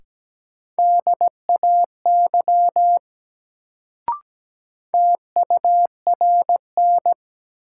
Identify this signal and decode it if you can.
Morse code, heard as a keyed tone, about 17 wpm, 700 Hz. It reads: DAY TURN
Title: Sentences from Top 200 Words - New Word Drill - 17wpm